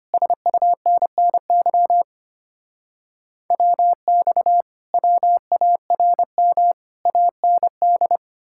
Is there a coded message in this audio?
Yes: Morse code from 0.1 to 8.2 s, heard as a keyed tone, about 33 wpm, 700 Hz. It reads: SUNNY WX WARM AND